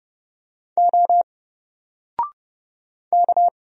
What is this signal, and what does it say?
Morse code, heard as a keyed tone, about 30 wpm, 700 Hz. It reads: O K
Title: Single Letter-Number 30wpm